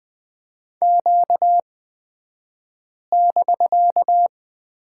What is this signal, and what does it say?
Morse code, heard as a keyed tone, about 20 wpm, spaced as for 5 wpm, 700 Hz. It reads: Q<BK>
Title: 249 Lesson 40d - BK - Two Characters 20wpm